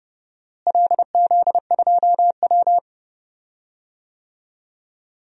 Morse code, heard as a keyed tone, about 30 wpm, 700 Hz. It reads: LZ2W
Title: DXCC-100 Call Signs - 4 Characters - Repeated 2x - 30wpm